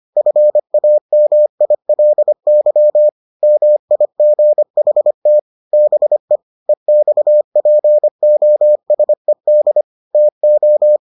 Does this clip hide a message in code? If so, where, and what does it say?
Yes: Morse code from 0.2 to 11.0 s, heard as a keyed tone, about 25 wpm, 600 Hz. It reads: FAMILY MIGHT BE EXPOSED TO